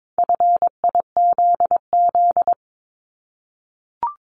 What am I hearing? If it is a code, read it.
Morse code, heard as a keyed tone, about 22 wpm, 700 Hz. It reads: FIZZ
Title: Multi-letter Phonemes 22wpm